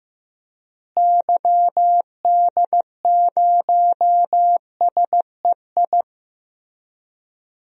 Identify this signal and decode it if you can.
Morse code, heard as a keyed tone, about 15 wpm, 700 Hz. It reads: YD0SEI
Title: DXCC-100 Call Signs - 6 Characters - Repeated 2x - 15wpm